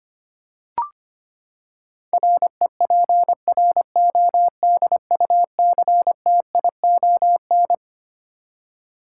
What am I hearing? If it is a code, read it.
Morse code, heard as a keyed tone, about 25 wpm, 700 Hz. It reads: REPRODUCTION
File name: Twelve Letter Words - 25wpm